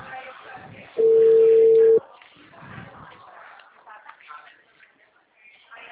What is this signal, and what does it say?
Ringback tone (China, 450 Hz), with speech.